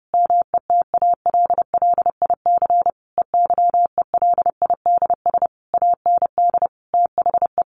Morse code, heard as a keyed tone, about 28 wpm, 700 Hz. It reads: METALLIC EYELIDS AND THE
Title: A Sound Of Thunder 30wpm